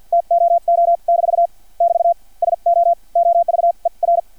Morse code, heard as a keyed tone, about 50 wpm, spaced as for 41 wpm, 680 Hz. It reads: TOO- <BT> SO OVER
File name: archived / CRSnet